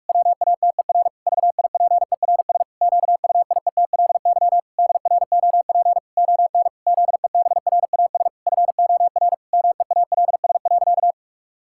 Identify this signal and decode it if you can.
Morse code, heard as a keyed tone, about 46 wpm, 700 Hz. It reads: WATER VIPERS QUIETLY DROP ON ZEBRAS FOR MEALS.
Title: Pangram Challenge v2 - 45wpm